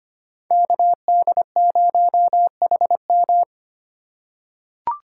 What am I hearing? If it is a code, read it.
Morse code, heard as a keyed tone, about 25 wpm, 700 Hz. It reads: KD0HM
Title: USA Call Signs - 5 Characters - 4x Character Spacing - 25wpm